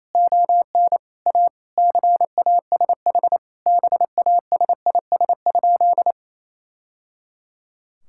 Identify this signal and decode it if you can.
Morse code, heard as a keyed tone, about 28 wpm, 700 Hz. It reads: ON A CASH BASIS?
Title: Sentences from Top 1000 Words - Review Part 5 - 28wpm